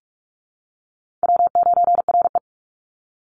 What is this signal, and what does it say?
Morse code, heard as a keyed tone, about 44 wpm, 700 Hz. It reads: W9RE
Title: Easy Call Signs - 45wpm